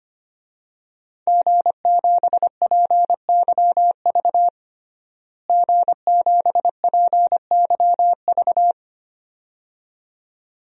Morse code, heard as a keyed tone, about 25 wpm, 700 Hz. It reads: G7PYV G7PYV
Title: DXCC-100 Call Signs - 5 Characters - Repeated 2x - 25wpm